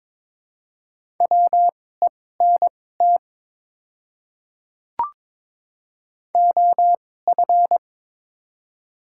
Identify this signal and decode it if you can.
Morse code, heard as a keyed tone, about 22 wpm, spaced as for 16 wpm, 700 Hz. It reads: WENT OF
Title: 2x Character Spacing - Top 200 Words - 22wpm